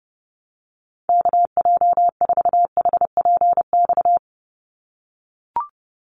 Morse code, heard as a keyed tone, about 30 wpm, 700 Hz. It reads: KJ4HPX
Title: USA Call Signs 30wpm